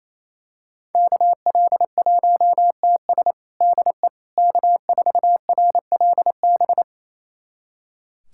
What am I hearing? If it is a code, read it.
Morse code, heard as a keyed tone, about 27 wpm, 700 Hz. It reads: KL1TS DE K4RLB
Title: USA Callsign DE Callsign 28wpm